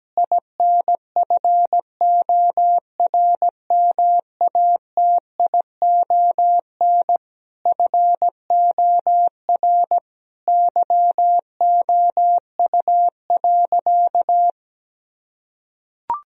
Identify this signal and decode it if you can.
Morse code, heard as a keyed tone, about 17 wpm, 700 Hz. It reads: INFORMATION FOR YOU.
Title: Sentences from Top 1000 Words - Review Part 3 - 17wpm